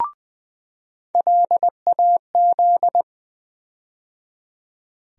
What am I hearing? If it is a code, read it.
Morse code, heard as a keyed tone, about 20 wpm, 700 Hz. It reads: LAZ
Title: USA Call Sign Postfixes - 3 Letters 20wpm